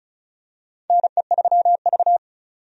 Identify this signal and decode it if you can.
Morse code, heard as a keyed tone, about 36 wpm, 700 Hz. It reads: NE3V
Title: USA Call Signs - 4 Characters - 35wpm